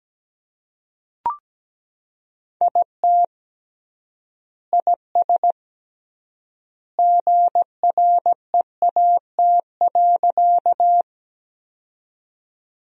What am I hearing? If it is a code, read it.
Morse code, heard as a keyed tone, about 17 wpm, 700 Hz. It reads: IT IS GREAT.
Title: 3x Word-Spacing - Sentences from Top 200 Words - 17wpm